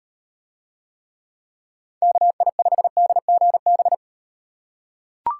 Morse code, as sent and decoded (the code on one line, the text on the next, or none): -.- .. ..... -.. --. -...
KI5DGB